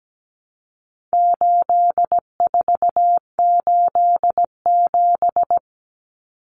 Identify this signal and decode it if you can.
Morse code, heard as a keyed tone, about 17 wpm, 700 Hz. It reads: 8487
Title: Common Cut Numbers - 4-Digit 17wpm